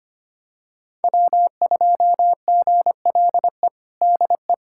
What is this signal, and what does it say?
Morse code, heard as a keyed tone, about 25 wpm, 700 Hz. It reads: W2GLE DE
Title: USA Callsign DE Callsign 25wpm